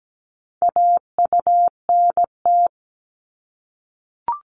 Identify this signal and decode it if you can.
Morse code, heard as a keyed tone, about 17 wpm, 700 Hz. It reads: AUNT
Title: Multi-letter Phonemes 17wpm